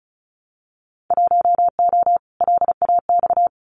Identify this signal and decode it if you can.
Morse code, heard as a keyed tone, about 36 wpm, 700 Hz. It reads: 1O LAX
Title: Winter Field Day Exchanges 35wpm